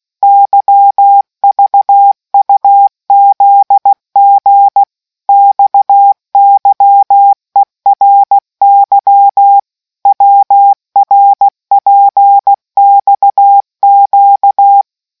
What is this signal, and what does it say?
Morse code, heard as a keyed tone, about 16 wpm, 800 Hz. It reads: YVUZG XYERY WRPXQ